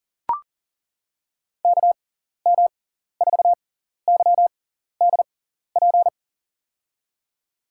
Morse code, heard as a keyed tone, about 40 wpm, spaced as for 14 wpm, 700 Hz. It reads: KM4YDP
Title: USA Call Signs - 6 Characters - 6x Character Spacing - 40wpm